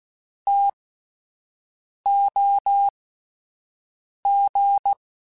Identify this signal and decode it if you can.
Morse code, heard as a keyed tone, about 16 wpm, spaced as for 5 wpm, 800 Hz. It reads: TOG
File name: T Prefixes / T mp3s